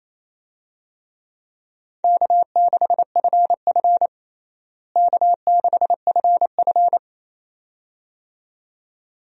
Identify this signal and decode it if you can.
Morse code, heard as a keyed tone, about 28 wpm, 700 Hz. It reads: K6FF K6FF
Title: USA Call Signs - 4 Characters - Repeated 2x - 28wpm